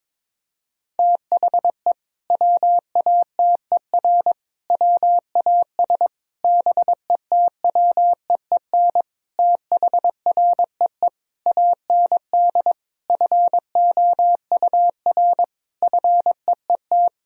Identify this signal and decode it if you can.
Morse code, heard as a keyed tone, about 22 wpm, 700 Hz. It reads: THE WATER WAS BETWEEN THREE AND FOUR FEET
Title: Sentences from Top 1000 Words - Review Part 3 - 22wpm